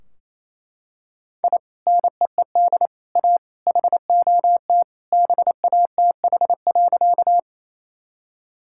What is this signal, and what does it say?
Morse code, heard as a keyed tone, about 28 wpm, 700 Hz. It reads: I NEED A HOT BATH.
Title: Sentences from Top 1600 Words 28wpm